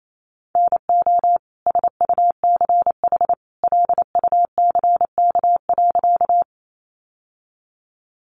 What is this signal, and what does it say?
Morse code, heard as a keyed tone, about 28 wpm, 700 Hz. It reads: NO SUCH LUCK.
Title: Sentences from Top 1500 Words 28wpm